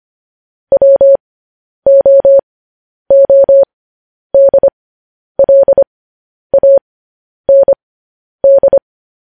Morse code, heard as a keyed tone, about 25 wpm, spaced as for 10 wpm, 550 Hz. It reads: WOODLAND
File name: podcast / News headlines-1x-25x10